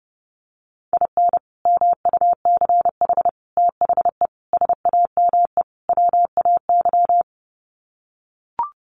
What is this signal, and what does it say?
Morse code, heard as a keyed tone, about 30 wpm, 700 Hz. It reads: IN MUCH THE SAME WAY
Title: Sets of 5 Words Encore 30wpm